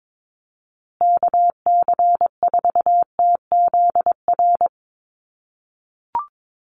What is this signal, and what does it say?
Morse code, heard as a keyed tone, about 22 wpm, 700 Hz. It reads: KC4TZR
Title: USA Call Signs Repeated 3x 22wpm